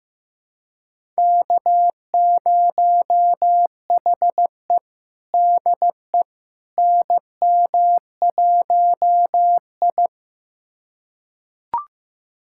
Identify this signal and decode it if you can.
Morse code, heard as a keyed tone, about 15 wpm, 700 Hz. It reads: K0HE DE NM1I
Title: USA Callsign DE Callsign - 4 Characters - 15wpm